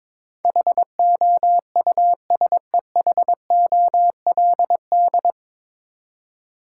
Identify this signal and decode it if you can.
Morse code, heard as a keyed tone, about 22 wpm, 700 Hz. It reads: HOUSEHOLD
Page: Morse Code Ninja practice 22wpm